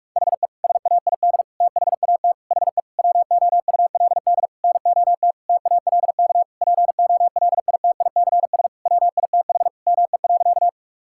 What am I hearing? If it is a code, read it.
Morse code, heard as a keyed tone, about 41 wpm, 700 Hz. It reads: HE SAID THAT HE WOULD NOT TALK POLITICS WITH ME.
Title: Sentences from Top 1000 Words - Review Part 6 - 45wpm